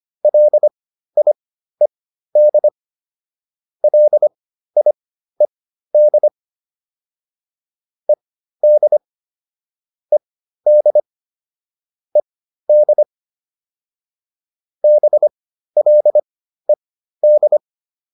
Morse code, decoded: LIED LIED ED ED ED BLED

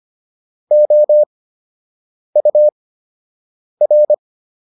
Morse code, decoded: OUR

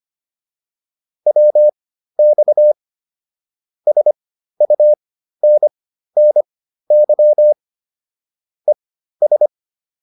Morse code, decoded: WX SUNNY ES

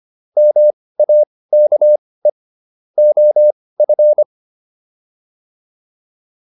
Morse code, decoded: MAKE OF

